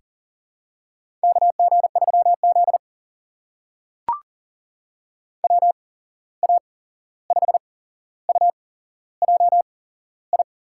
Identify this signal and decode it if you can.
Morse code, heard as a keyed tone, about 40 wpm, spaced as for 15 wpm, 700 Hz. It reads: KG3Z WA5UJI
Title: USA Call Signs - 8x Character Spacing - 40wpm